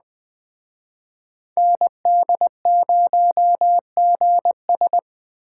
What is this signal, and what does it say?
Morse code, heard as a keyed tone, about 21 wpm, 700 Hz. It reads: ND0GS
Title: Speed Racing - USA Call Signs 17wpm